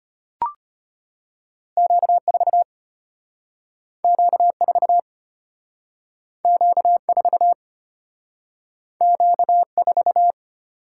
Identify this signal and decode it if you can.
Morse code, heard as a keyed tone, about 32 wpm, 700 Hz. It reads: Q4 Q4 Q4 Q4